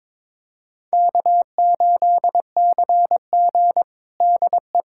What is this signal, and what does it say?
Morse code, heard as a keyed tone, about 22 wpm, 700 Hz. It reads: K8CG DE
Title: USA Callsign DE Callsign - 4 Characters - 22wpm